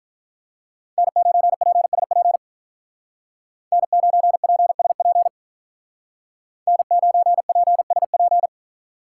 Morse code, heard as a keyed tone, about 43 wpm, 700 Hz. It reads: N9PSP N9PSP N9PSP
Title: Speed Racing - USA 1x3 Call Signs 35wpm